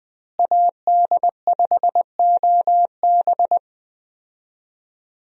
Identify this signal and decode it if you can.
Morse code, heard as a keyed tone, about 20 wpm, 700 Hz. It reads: AD5OB
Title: USA Call Signs - 5 Characters - Repeated 2x - 20wpm